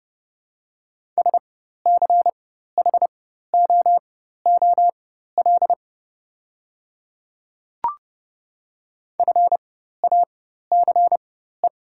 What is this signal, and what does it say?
Morse code, heard as a keyed tone, about 30 wpm, spaced as for 14 wpm, 690 Hz. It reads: SCHOOL FACE
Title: 4x Character Spacing - Top 500 Words - 30wpm